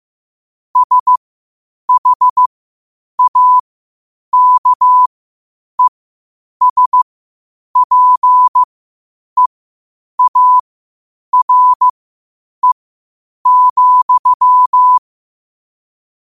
Morse code, heard as a keyed tone, about 15 wpm, spaced as for 8 wpm, 1000 Hz. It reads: SHAKESPEARE,
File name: qotdfiles / 8wpm